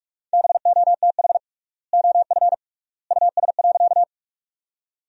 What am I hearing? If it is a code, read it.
Morse code, heard as a keyed tone, about 46 wpm, 700 Hz. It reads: BOTH OF US.
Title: 3x Word-Spacing - Sentences from Top 300 Words - 45wpm